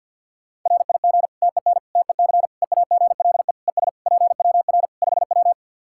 Morse code, heard as a keyed tone, about 53 wpm, 700 Hz. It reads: RIG TEN TEC EAGLE ES PWR 5W
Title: Practice QSO - Exchange 2 - Info - 50wpm